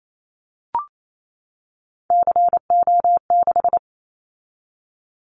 Morse code, heard as a keyed tone, about 28 wpm, 700 Hz. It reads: CO6
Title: Callsign Prefixes for Top 100 DXCC - 28wpm